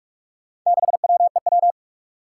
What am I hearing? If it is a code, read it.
Morse code, heard as a keyed tone, about 46 wpm, 700 Hz. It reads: 6WEW
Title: Four Challenging Letter-Number Combinations 45wpm